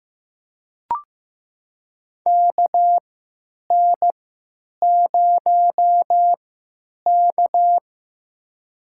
Morse code, heard as a keyed tone, about 15 wpm, spaced as for 9 wpm, 700 Hz. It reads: KN0K
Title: USA Call Signs - 4 Characters - 3x Character Spacing - 15wpm